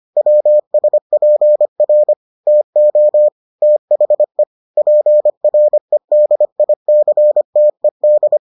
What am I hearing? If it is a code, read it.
Morse code, heard as a keyed tone, about 25 wpm, 600 Hz. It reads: WSPR TO THE PREDICTED